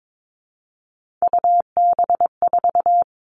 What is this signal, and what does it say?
Morse code, heard as a keyed tone, about 22 wpm, 700 Hz. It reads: UB4
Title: All Cut Numbers - 3-Digit 22wpm